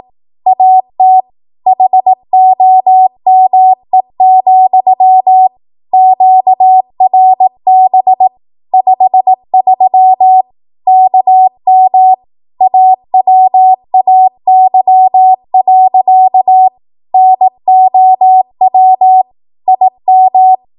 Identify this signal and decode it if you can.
Morse code, heard as a keyed tone, about 18 wpm, 750 Hz. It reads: AT HOME, QRB 53 KM AWAY. NOW IM
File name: Morse / 18WPM